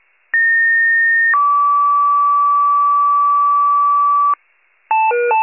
Two-tone page, heard as a one-second tone, then a three-second tone, about 1850 then 1150 Hz, with noise. It is heard from 0.3 to 4.3 s.